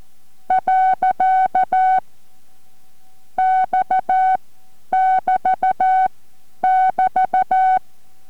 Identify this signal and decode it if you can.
Morse code, heard as a keyed tone, about 14 wpm, spaced as for 5 wpm, 750 Hz. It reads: .X <BT> <BT>
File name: MorseCode-5WPM 4